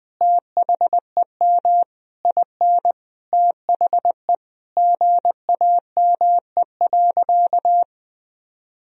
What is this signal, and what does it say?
Morse code, heard as a keyed tone, about 20 wpm, 700 Hz. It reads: THEM IN THE GAME.
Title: Sentences from Top 1000 Words - Review Part 3 - 20wpm